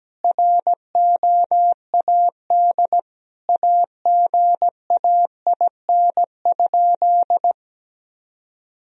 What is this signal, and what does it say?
Morse code, heard as a keyed tone, about 17 wpm, 700 Hz. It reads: ROAD AGAIN?
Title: Sentences from Top 1000 Words - Review Part 2 - 17wpm